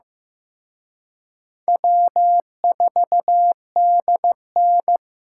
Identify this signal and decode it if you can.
Morse code, heard as a keyed tone, about 18 wpm, 700 Hz. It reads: W4DN
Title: Speed Racing - USA 1x2 Call Signs 15wpm